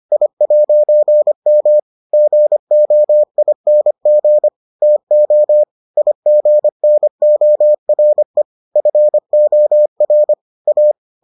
Morse code, heard as a keyed tone, about 25 wpm, 600 Hz. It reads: I'M GOING TO IGNORE FOR A